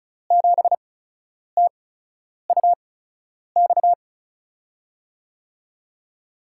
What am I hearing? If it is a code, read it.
Morse code, heard as a keyed tone, about 36 wpm, spaced as for 10 wpm, 700 Hz. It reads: 7TUX
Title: USA Call Signs - 5 Characters - 8x Character Spacing - 35wpm